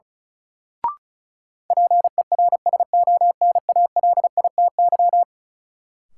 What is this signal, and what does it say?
Morse code, heard as a keyed tone, about 34 wpm, 700 Hz. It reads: PERSONALITY
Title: Sentences from Top 1400 Words - New Word Drill Alt - 35wpm